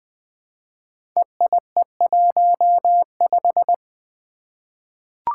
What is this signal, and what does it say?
Morse code, heard as a keyed tone, about 20 wpm, 700 Hz. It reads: EIE15
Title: Five Challenging Letter-Number Combinations 20wpm